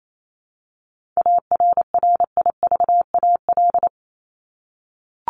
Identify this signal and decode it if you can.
Morse code, heard as a keyed tone, about 28 wpm, 700 Hz. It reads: ARRIVAL
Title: Top 1751-2000 Words 28wpm